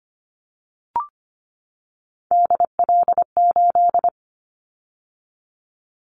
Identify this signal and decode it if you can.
Morse code, heard as a keyed tone, about 25 wpm, 700 Hz. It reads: DL8
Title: Callsign Prefixes for Top 50 DXCC - 25wpm